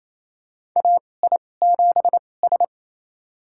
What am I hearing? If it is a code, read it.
Morse code, heard as a keyed tone, about 28 wpm, spaced as for 20 wpm, 700 Hz. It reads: AI7S
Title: USA Call Signs - 4 Characters - 2x Character Spacing - 28wpm